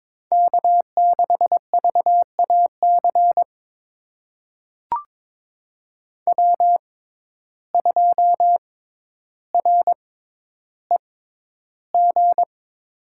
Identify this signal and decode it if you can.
Morse code, heard as a keyed tone, about 22 wpm, spaced as for 11 wpm, 700 Hz. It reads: K6VAC W2REG
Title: USA Call Signs - 5 Characters - 6x Character Spacing - 22wpm